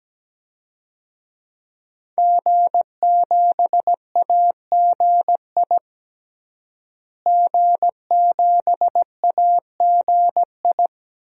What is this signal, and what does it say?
Morse code, heard as a keyed tone, about 17 wpm, 700 Hz. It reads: G7AGI G7AGI
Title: DXCC-100 Call Signs - 5 Characters - Repeated 3x - 17wpm